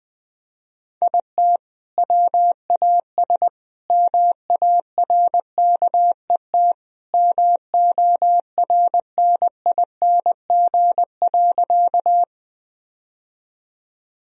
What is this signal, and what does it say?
Morse code, heard as a keyed tone, about 21 wpm, 700 Hz. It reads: IT WAS MARKET MORNING.
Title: Sentences from Top 1000 Words - Review Part 3 - 20wpm